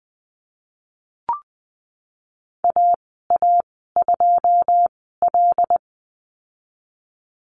Morse code, heard as a keyed tone, about 20 wpm, spaced as for 15 wpm, 700 Hz. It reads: AA2L
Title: 179 Lesson 29g - Slash - Call Signs 20wpm